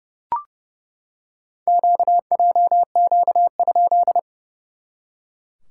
Morse code, heard as a keyed tone, about 31 wpm, 700 Hz. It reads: QJQ?
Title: SOTA QSO Practice 30wpm